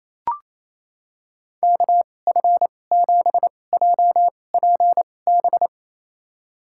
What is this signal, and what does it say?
Morse code, heard as a keyed tone, about 28 wpm, spaced as for 20 wpm, 700 Hz. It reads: KF7JPB
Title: USA Call Signs - 6 Characters - 2x Character Spacing - 28wpm